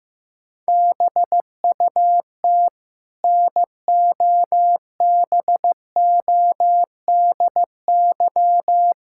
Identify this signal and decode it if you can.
Morse code, heard as a keyed tone, about 15 wpm, 700 Hz. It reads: BUT NOBODY
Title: Sentences from Top 1600 Words 15wpm